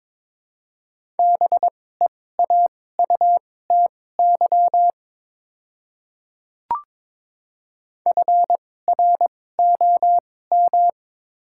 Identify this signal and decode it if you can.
Morse code, heard as a keyed tone, about 22 wpm, spaced as for 16 wpm, 700 Hz. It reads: BEAUTY FROM
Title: 2x Character Spacing - Top 500 Words - 22wpm